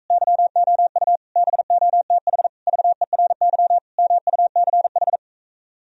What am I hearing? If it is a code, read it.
Morse code, heard as a keyed tone, about 45 wpm, 700 Hz. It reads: YOU BOTH VERY MUCH